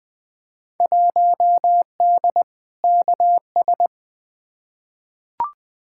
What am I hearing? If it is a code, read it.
Morse code, heard as a keyed tone, about 20 wpm, 700 Hz. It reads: 1D KS
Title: Field Day Exchanges - 20wpm